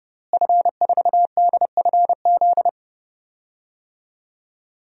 Morse code, heard as a keyed tone, about 31 wpm, 700 Hz. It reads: F4DFZ
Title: DXCC-100 Call Signs - 5 Characters - Repeated 2x - 30wpm